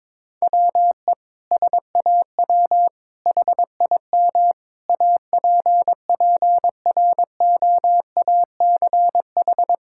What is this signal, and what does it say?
Morse code, heard as a keyed tone, about 21 wpm, 700 Hz. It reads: WE SAW HIM APPROACH
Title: Sentences from Top 1000 Words - Review Part 6 - 22wpm